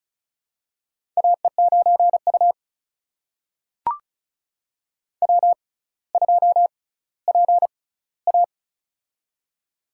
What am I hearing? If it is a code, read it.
Morse code, heard as a keyed tone, about 36 wpm, spaced as for 18 wpm, 700 Hz. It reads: AE9U W2PA
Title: USA Call Signs - 4 Characters - 6x Character Spacing - 35wpm